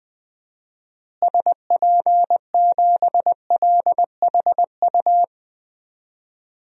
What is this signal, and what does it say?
Morse code, heard as a keyed tone, about 21 wpm, 700 Hz. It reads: SP7LHU
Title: DXCC-100 Call Signs - 6 Characters - Repeated 3x - 20wpm